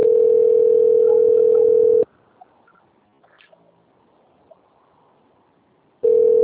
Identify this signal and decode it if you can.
Ringback tone (North America), 440 + 480 Hz.
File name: out- - - - - .6732